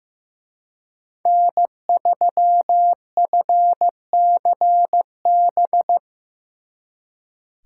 Morse code, heard as a keyed tone, about 18 wpm, 700 Hz. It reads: N3FCB
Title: Speed Racing - USA 1x3 Call Signs 15wpm